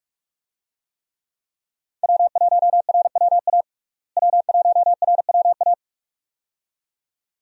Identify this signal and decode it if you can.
Morse code, heard as a keyed tone, about 46 wpm, 700 Hz. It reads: W1RWA W1RWA
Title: USA Call Signs - 5 Characters - Repeated 2x - 45wpm